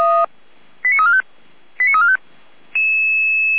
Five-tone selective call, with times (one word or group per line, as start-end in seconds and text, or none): 0.8-1.2 89345
1.8-2.2 89345